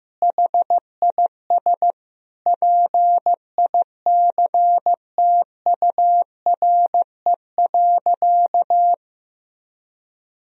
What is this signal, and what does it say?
Morse code, heard as a keyed tone, about 15 wpm, 700 Hz. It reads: HIS PICTURE.
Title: Sentences from Top 1000 Words - Review Part 1 - 15wpm